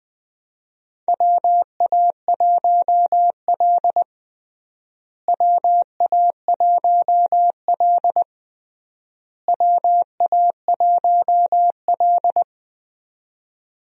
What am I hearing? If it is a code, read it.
Morse code, heard as a keyed tone, about 20 wpm, 700 Hz. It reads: WA1L WA1L WA1L
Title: USA Call Signs - 4 Characters - Repeated 3x - 20wpm